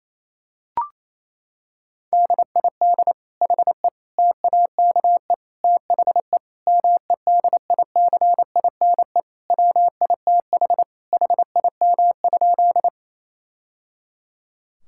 Morse code, heard as a keyed tone, about 28 wpm, 700 Hz. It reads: DID HE TAKE THE MEDICINE WITH HIM?
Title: Sentences from Top 1000 Words - Review Part 6 - 28wpm